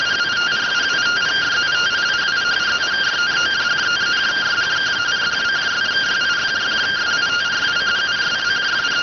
Two-tone teleprinter signal, about 170 Hz shift.